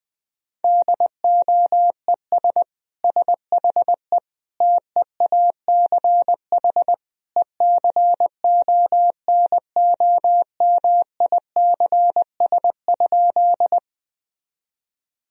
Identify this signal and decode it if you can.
Morse code, heard as a keyed tone, about 20 wpm, 700 Hz. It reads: DOES SHE TEACH ECONOMICS?
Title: Sentences from Top 1000 Words - Review Part 3 - 20wpm